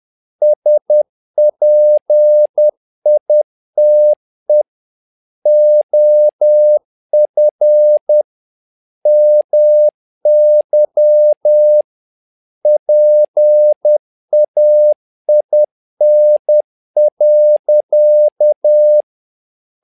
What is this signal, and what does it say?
Morse code, heard as a keyed tone, about 10 wpm, 600 Hz. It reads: SPITE OF MY PAIN.
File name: learning center / tower trouble / tower trouble 10wpm